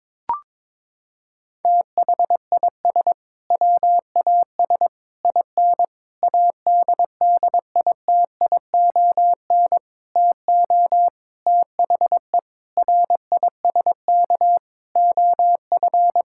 Morse code, heard as a keyed tone, about 22 wpm, 700 Hz. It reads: THIS WAS IN ADDITION TO THE RISK OF
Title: Sentences from Top 1000 Words - Review Part 6 - 22wpm